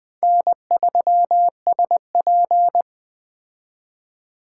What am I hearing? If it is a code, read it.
Morse code, heard as a keyed tone, about 20 wpm, 700 Hz. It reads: N3SP
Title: Speed Racing - USA Call Signs 17wpm